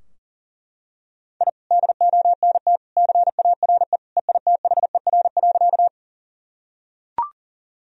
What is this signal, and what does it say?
Morse code, heard as a keyed tone, about 38 wpm, 700 Hz. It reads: I DONT CARE EITHER.
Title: Sentences from Top 1000 Words - Review Part 4 - 40wpm